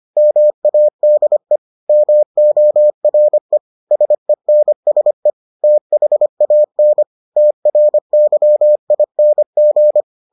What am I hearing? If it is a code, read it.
Morse code, heard as a keyed tone, about 25 wpm, 600 Hz. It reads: MADE MORE SENSE THAN TRYING